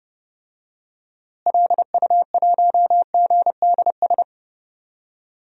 Morse code, heard as a keyed tone, about 29 wpm, 700 Hz. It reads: LU1GDS
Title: DXCC-100 Call Signs - 6 Characters - Repeated 3x - 30wpm